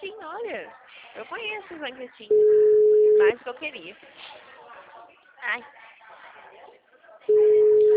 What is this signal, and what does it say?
Ringback tone (Europe, 425 Hz), with speech.
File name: out- -2002- - - .473